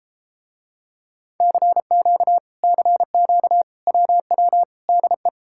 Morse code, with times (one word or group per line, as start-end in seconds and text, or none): 1.4-2.4 CQ
2.6-3.6 CQ
3.9-4.6 WW
4.9-5.3 DE